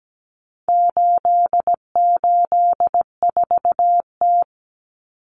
Morse code, heard as a keyed tone, about 17 wpm, 700 Hz. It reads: 884T
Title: Common Cut Numbers - 4-Digit 17wpm